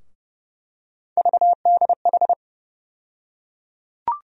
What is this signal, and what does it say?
Morse code, heard as a keyed tone, about 31 wpm, 700 Hz. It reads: VDH
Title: USA Call Sign Postfixes - 3 Letters 30wpm